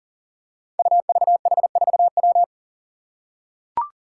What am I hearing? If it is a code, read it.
Morse code, heard as a keyed tone, about 40 wpm, 700 Hz. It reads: UVH4W
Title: Five Challenging Letter-Number Combinations 40wpm